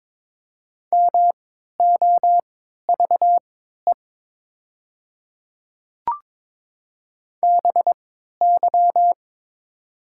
Morse code, heard as a keyed tone, about 22 wpm, spaced as for 12 wpm, 700 Hz. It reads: MOVE BY
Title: 3x Character Spacing - Top 300 Words - 22wpm